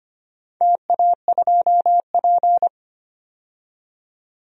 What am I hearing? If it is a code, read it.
Morse code, heard as a keyed tone, about 25 wpm, 700 Hz. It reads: TA2P
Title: DXCC-100 Call Signs - 4 Characters - 25wpm